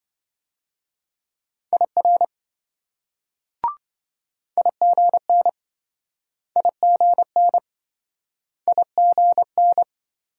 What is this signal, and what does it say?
Morse code, heard as a keyed tone, about 26 wpm, 700 Hz. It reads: IR IGN IGN IGN